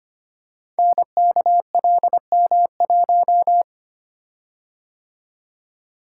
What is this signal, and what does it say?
Morse code, heard as a keyed tone, about 25 wpm, 700 Hz. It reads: NKLM1
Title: Code Groups - 5 Letter:Numbers - 25wpm